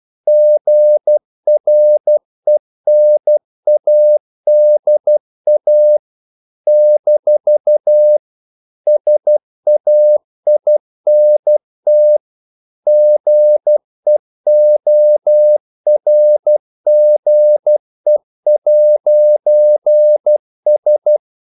Morse code, heard as a keyed tone, about 12 wpm, 600 Hz. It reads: GRENADA - SAINT GEORGE'S
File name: CW / Practice / NationalCapitals12